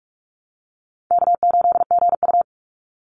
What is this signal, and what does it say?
Morse code, heard as a keyed tone, about 44 wpm, 700 Hz. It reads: K8GU
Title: Easy Call Signs Repeated 2x - 45wpm